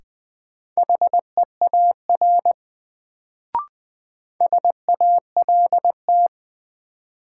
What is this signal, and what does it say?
Morse code, heard as a keyed tone, about 20 wpm, 700 Hz. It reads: HEAR SALT
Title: Sets of 5 Words - New Word Drill - 20wpm